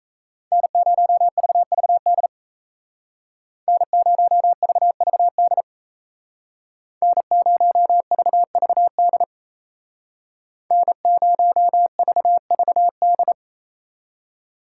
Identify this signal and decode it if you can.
Morse code, heard as a keyed tone, about 37 wpm, 700 Hz. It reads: N0VVD N0VVD N0VVD N0VVD